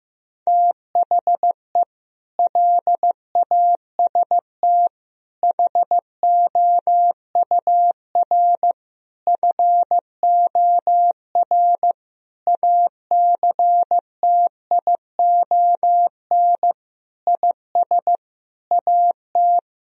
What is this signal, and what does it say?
Morse code, heard as a keyed tone, about 16 wpm, 700 Hz. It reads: THE LAST HOUR FOR ACTION IS AT
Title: Sentences from Top 1000 Words - Review Part 4 - 15wpm